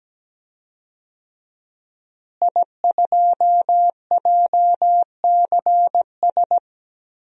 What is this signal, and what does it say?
Morse code, heard as a keyed tone, about 17 wpm, 700 Hz. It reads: I2JCS
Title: DXCC-100 Call Signs - 5 Characters - Repeated 2x - 17wpm